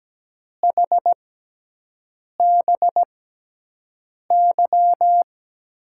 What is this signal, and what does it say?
Morse code, heard as a keyed tone, about 17 wpm, spaced as for 6 wpm, 700 Hz. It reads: HBY
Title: USA Call Signs - 5 Characters - 6x Character Spacing - 17wpm